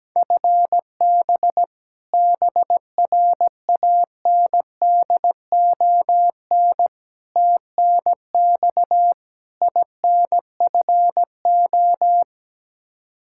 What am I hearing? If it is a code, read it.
Morse code, heard as a keyed tone, about 17 wpm, 700 Hz. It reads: FB BRANDON TNX INFO
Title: Practice QSO - Exchange 3 - More Info - 17wpm